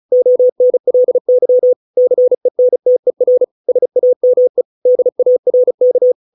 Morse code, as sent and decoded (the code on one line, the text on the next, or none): --- -. .-.. -.-- / -.-. -. - . .-. / ... .- -- . / -.. .- .-. -.-
ONLY CENTER SAME DARK